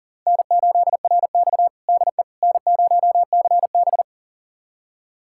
Morse code, heard as a keyed tone, about 37 wpm, 700 Hz. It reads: N8RX DE N0CB
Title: USA Callsign DE Callsign - 4 Characters - 40wpm